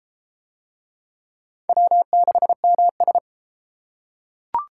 Morse code, heard as a keyed tone, about 29 wpm, 700 Hz. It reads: W6MS